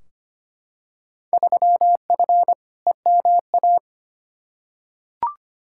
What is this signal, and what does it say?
Morse code, heard as a keyed tone, about 25 wpm, 700 Hz. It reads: 3F EMA
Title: Field Day Exchanges Encore - 25wpm